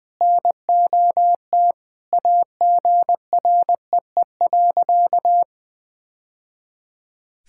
Morse code, heard as a keyed tone, about 20 wpm, 700 Hz. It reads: NOT AGREE.
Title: Sentences with 2-Word Phrases - 20wpm